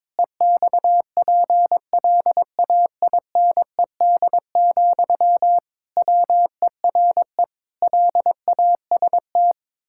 Morse code, heard as a keyed tone, about 22 wpm, 700 Hz. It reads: EXPLAINED, WERE LAST
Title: Pangram Challenge v2 - 22wpm